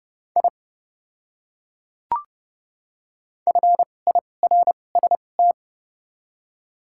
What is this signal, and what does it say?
Morse code, heard as a keyed tone, about 30 wpm, spaced as for 22 wpm, 700 Hz. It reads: I FIRST
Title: 2x Character Spacing - Top 100 Words - 30wpm